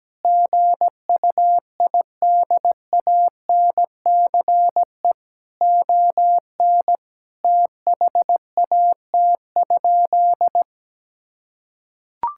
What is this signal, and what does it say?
Morse code, heard as a keyed tone, about 17 wpm, 700 Hz. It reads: GUIDANCE ON THAT?
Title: Sentences from Top 1600 Words 17wpm